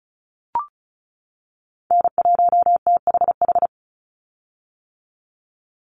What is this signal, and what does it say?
Morse code, heard as a keyed tone, about 34 wpm, 700 Hz. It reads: N1THH
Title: USA Call Signs 35wpm